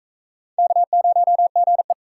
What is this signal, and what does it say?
Morse code, heard as a keyed tone, about 40 wpm, 700 Hz. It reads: K0GE